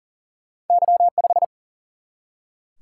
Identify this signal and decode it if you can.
Morse code, heard as a keyed tone, about 40 wpm, 700 Hz. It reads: Y 5
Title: Single-Letter Single-Number 40wpm